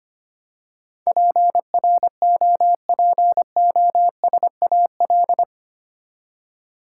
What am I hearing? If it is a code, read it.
Morse code, heard as a keyed tone, about 25 wpm, 700 Hz. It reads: PROPOSAL
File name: Eight Letter Words - 25wpm